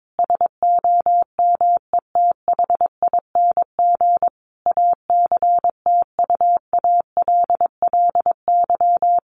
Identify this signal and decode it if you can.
Morse code, heard as a keyed tone, about 22 wpm, 700 Hz. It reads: SOMETHING ACTUALLY
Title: Sentences from Top 1500 Words 22wpm